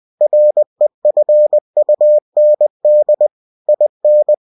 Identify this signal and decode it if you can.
Morse code, heard as a keyed tone, about 20 wpm, 600 Hz. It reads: REFUND IN
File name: cw-books / winnie-the-pooh / winnie-20wpm-page-0119